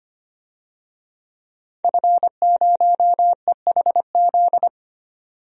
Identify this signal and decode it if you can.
Morse code, heard as a keyed tone, about 25 wpm, 700 Hz. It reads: F0EHZ
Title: DXCC-100 Call Signs - 5 Characters - 25wpm